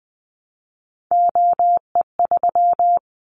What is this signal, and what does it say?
Morse code, heard as a keyed tone, about 19 wpm, 700 Hz. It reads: OE3
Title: Callsign Prefixes for Top 25 DXCC - 20wpm